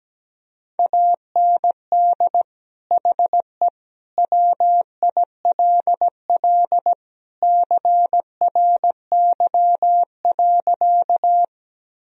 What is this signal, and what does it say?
Morse code, heard as a keyed tone, about 17 wpm, 700 Hz. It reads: AND HE WILL CRY.
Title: Sentences from Top 1000 Words - Review Part 2 - 17wpm